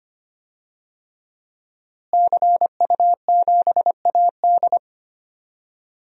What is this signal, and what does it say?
Morse code, heard as a keyed tone, about 25 wpm, 700 Hz. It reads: CU7AD
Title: DXCC-100 Call Signs - 5 Characters - Repeated 3x - 25wpm